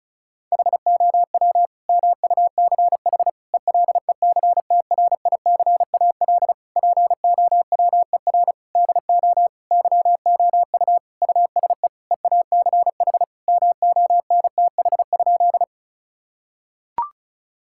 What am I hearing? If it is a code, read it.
Morse code, heard as a keyed tone, about 34 wpm, 700 Hz. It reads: HOW MUCH ELECTRICAL POWER DO YOU USE EACH MONTH?
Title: Sentences from Top 1000 Words - Review Part 6 - 35wpm